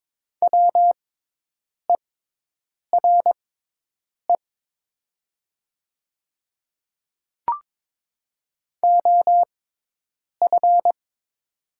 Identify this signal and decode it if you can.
Morse code, heard as a keyed tone, about 22 wpm, spaced as for 8 wpm, 700 Hz. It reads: WERE OF
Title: 6x Character Spacing - Top 100 Words - 22wpm